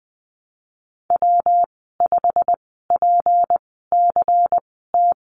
Morse code, heard as a keyed tone, about 20 wpm, spaced as for 15 wpm, 700 Hz. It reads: W5PCT
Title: 173 Lesson 28g - Number 9 - Call Signs 20wpm